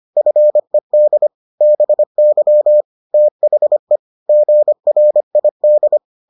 Morse code, heard as a keyed tone, about 25 wpm, 600 Hz. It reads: FED BY THE GRID